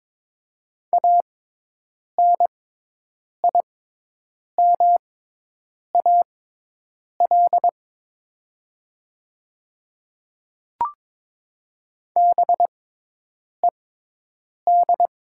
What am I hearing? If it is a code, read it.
Morse code, heard as a keyed tone, about 22 wpm, spaced as for 8 wpm, 700 Hz. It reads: ANIMAL BED
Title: 6x Character Spacing - Top 500 Words - 22wpm